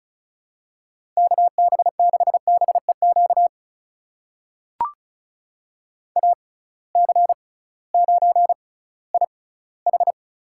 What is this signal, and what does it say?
Morse code, heard as a keyed tone, about 36 wpm, spaced as for 19 wpm, 700 Hz. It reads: KB6BEQ AC9IH
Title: USA Call Signs - 6x Character Spacing - 35wpm